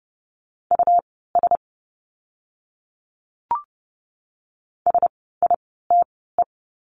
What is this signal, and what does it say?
Morse code, heard as a keyed tone, about 30 wpm, spaced as for 17 wpm, 700 Hz. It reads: US SITE
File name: Lesson 17f - Letter u - Words 30wpm